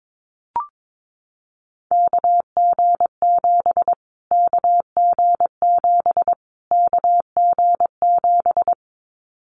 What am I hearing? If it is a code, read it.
Morse code, heard as a keyed tone, about 22 wpm, 700 Hz. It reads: KG7 KG7 KG7
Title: Callsign Prefixes for Top 25 DXCC - Repeated 3x - 22wpm